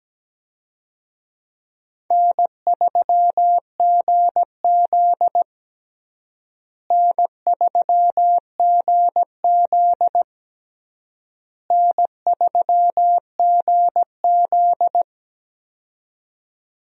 Morse code, heard as a keyed tone, about 17 wpm, 700 Hz. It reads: N3GZ N3GZ N3GZ
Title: USA Call Signs - 4 Characters - Repeated 3x - 17wpm